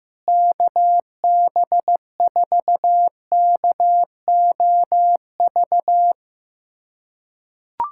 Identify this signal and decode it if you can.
Morse code, heard as a keyed tone, about 15 wpm, 700 Hz. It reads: KB4KOV
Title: USA Callsign DE Callsign - 6 Characters - 15wpm